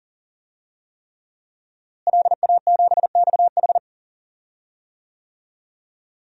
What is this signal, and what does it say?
Morse code, heard as a keyed tone, about 40 wpm, 700 Hz. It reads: LA7XH
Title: DXCC-100 Call Signs - 5 Characters - Repeated 3x - 40wpm